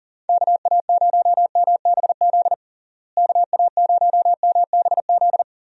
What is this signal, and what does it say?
Morse code, heard as a keyed tone, about 40 wpm, 700 Hz. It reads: KA0MBZ KA0MBZ
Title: USA Call Signs - 6 Characters - Repeated 2x - 40wpm